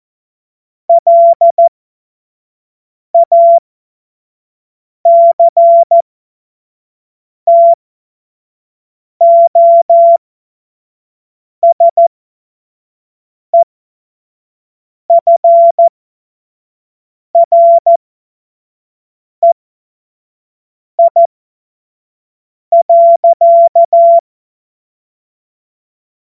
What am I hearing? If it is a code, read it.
Morse code, heard as a keyed tone, about 14 wpm, spaced as for 5 wpm, 680 Hz. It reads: LACTOSEFREI.
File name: Morsekurs / Witze / Witz 22. Juni 2021, 25 F70